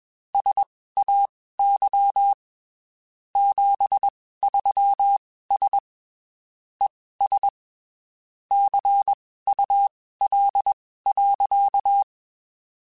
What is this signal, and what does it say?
Morse code, heard as a keyed tone, about 21 wpm, spaced as for 15 wpm, 800 Hz. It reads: SAY 73S ES CUL.
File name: (spaced) / QSOs Spaced-mp3s / WA7QOX 2 @20 sp